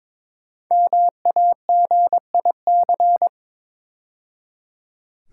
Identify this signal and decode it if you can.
Morse code, heard as a keyed tone, about 22 wpm, 700 Hz. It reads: MAGIC
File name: 5 Letter Words - First 3 Letters Given - 22wpm